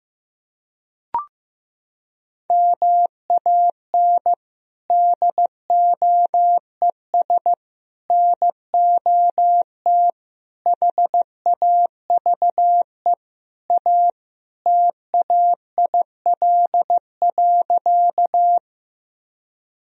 Morse code, heard as a keyed tone, about 16 wpm, 700 Hz. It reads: MAN DOES NOT HAVE A TAIL.
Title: Sentences from Top 1000 Words - Review Part 3 - 15wpm